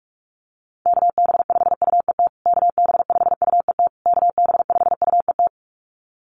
Morse code, heard as a keyed tone, about 44 wpm, 700 Hz. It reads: KB5UET KB5UET KB5UET
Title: USA Call Signs Repeated 3x 45wpm